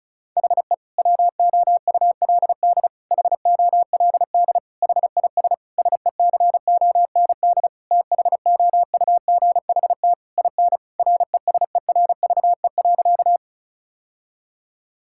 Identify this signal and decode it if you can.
Morse code, heard as a keyed tone, about 34 wpm, 700 Hz. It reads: HE WOULD HOLD HIS SECOND THOUGHT IN RESERVE.
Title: Sentences from Top 1000 Words - Review Part 6 - 35wpm